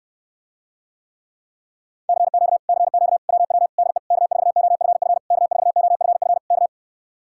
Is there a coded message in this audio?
Yes: Morse code from 2.1 to 6.7 s, heard as a keyed tone, about 67 wpm, 700 Hz. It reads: CQ CQ WW DE K4OFF K4OFF K